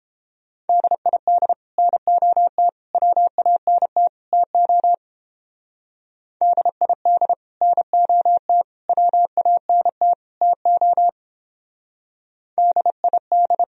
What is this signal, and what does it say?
Morse code, heard as a keyed tone, about 29 wpm, 700 Hz. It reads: DID NOT WANT TO DID NOT WANT TO DID